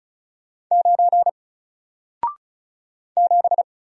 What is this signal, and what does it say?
Morse code, heard as a keyed tone, about 33 wpm, 700 Hz. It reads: 9 7
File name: Single-Letter-Number-35wpm